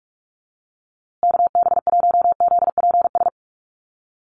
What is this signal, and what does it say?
Morse code, heard as a keyed tone, about 44 wpm, 700 Hz. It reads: KB1ZPS
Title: USA Call Signs Repeated 2x 45wpm